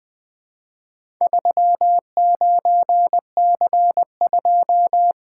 Morse code, heard as a keyed tone, about 20 wpm, 700 Hz. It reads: 39C2
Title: Four Letter-Number Combinations - 20wpm